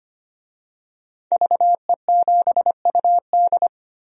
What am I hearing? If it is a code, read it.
Morse code, heard as a keyed tone, about 25 wpm, 700 Hz. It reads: VE7UD
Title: DXCC-100 Call Signs - 5 Characters - 25wpm